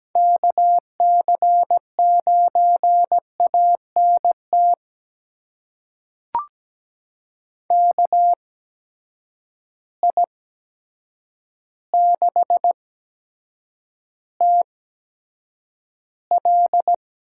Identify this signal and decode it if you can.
Morse code, heard as a keyed tone, about 17 wpm, spaced as for 8 wpm, 700 Hz. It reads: KC9ANT KI6TL
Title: USA Call Signs - 8x Character Spacing - 17wpm